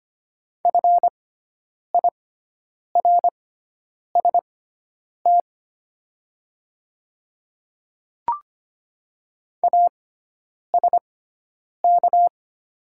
Morse code, heard as a keyed tone, about 25 wpm, spaced as for 9 wpm, 700 Hz. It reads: FIRST ASK
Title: 6x Character Spacing - Top 200 Words - 25wpm